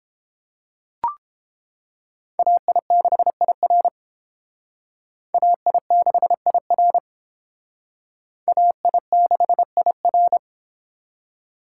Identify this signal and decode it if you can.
Morse code, heard as a keyed tone, about 30 wpm, 700 Hz. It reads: AI6IR AI6IR AI6IR